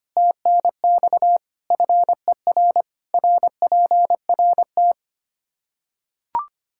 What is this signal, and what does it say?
Morse code, heard as a keyed tone, about 25 wpm, 700 Hz. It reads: TNX FER RPRT
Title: K1USN SST QSO Practice 25wpm